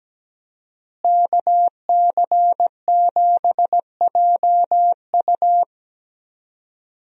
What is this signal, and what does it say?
Morse code, heard as a keyed tone, about 17 wpm, 700 Hz. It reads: KC7JU
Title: USA Call Signs - 5 Characters - Repeated 2x - 17wpm